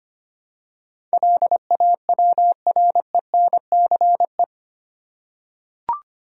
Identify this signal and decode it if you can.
Morse code, heard as a keyed tone, about 25 wpm, 700 Hz. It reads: LAWRENCE
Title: USA Common Names - 25wpm